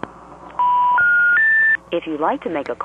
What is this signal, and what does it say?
Special information tone, with speech.
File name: TurtleCreekEWSD